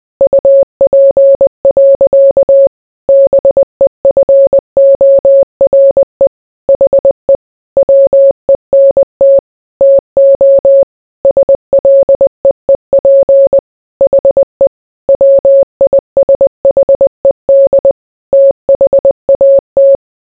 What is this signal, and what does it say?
Morse code, heard as a keyed tone, about 20 wpm, 550 Hz. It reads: UP. BEFORE HE WENT TO SLEEP HE WISHED THAT